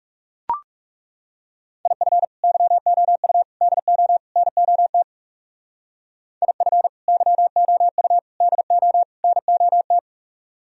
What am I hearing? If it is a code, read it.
Morse code, heard as a keyed tone, about 36 wpm, 700 Hz. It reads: IF YOU DO NOT IF YOU DO NOT
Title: Speed-Racing - Sets of 4 Words 30wpm